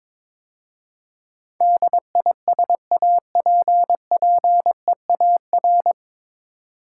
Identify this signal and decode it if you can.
Morse code, heard as a keyed tone, about 23 wpm, 700 Hz. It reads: DISAPPEAR
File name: 9 Letter Words - First 3 Letters Given - 22wpm